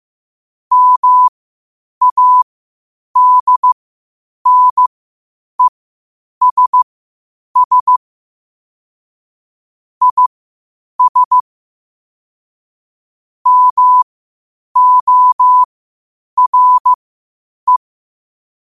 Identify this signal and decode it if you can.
Morse code, heard as a keyed tone, about 15 wpm, spaced as for 8 wpm, 1000 Hz. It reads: MADNESS IS MORE